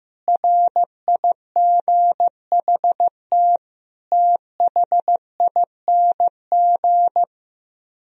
Morse code, heard as a keyed tone, about 16 wpm, 700 Hz. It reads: RIGHT THING
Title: Speed-Racing - Sets of 5 Words 15wpm